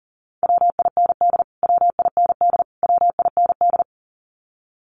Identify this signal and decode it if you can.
Morse code, heard as a keyed tone, about 40 wpm, 700 Hz. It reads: WIND WIND WIND
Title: Top 500 Words Repeated 40wpm